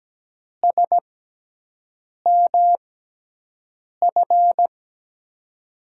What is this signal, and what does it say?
Morse code, heard as a keyed tone, about 17 wpm, spaced as for 6 wpm, 700 Hz. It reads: SMF
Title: USA Call Signs - 6x Character Spacing - 17wpm